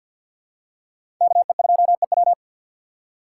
Morse code, heard as a keyed tone, about 50 wpm, 700 Hz. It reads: KE2EW
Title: USA Call Signs - 5 Characters - 6x Character Spacing - 50wpm